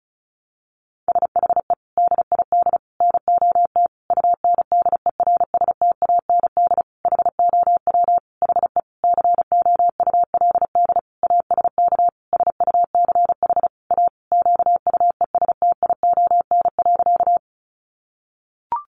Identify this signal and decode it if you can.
Morse code, heard as a keyed tone, about 34 wpm, 700 Hz. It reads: SHE DID NOT UNDERSTAND HOW HE COULD ASK SUCH A QUESTION.
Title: Sentences from Top 600 Words 35wpm